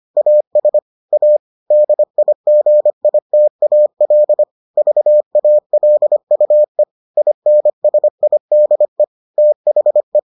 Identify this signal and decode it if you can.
Morse code, heard as a keyed tone, about 25 wpm, 600 Hz. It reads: AS A DIGITAL VALUE INSIDE THE